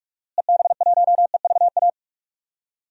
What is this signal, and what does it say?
Morse code, heard as a keyed tone, about 43 wpm, 700 Hz. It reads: EB1EVA
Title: DXCC-100 Call Signs - 6 Characters - Repeated 2x - 45wpm